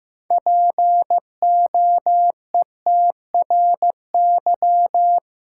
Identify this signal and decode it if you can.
Morse code, heard as a keyed tone, about 15 wpm, 700 Hz. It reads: POETRY
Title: Six Letter Words - 15wpm